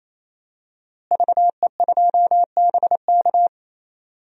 Morse code, heard as a keyed tone, about 28 wpm, 700 Hz. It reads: VE2BK
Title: DXCC-100 Call Signs - 5 Characters - Repeated 3x - 28wpm